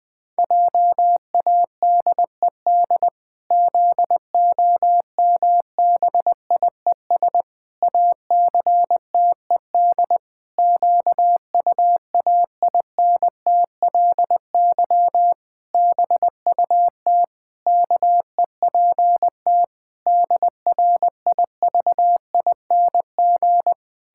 Morse code, heard as a keyed tone, about 21 wpm, 700 Hz. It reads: JADED ZOMBIES ACTED QUAINTLY BUT KEPT DRIVING
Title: Pangram Challenge v2 - 20wpm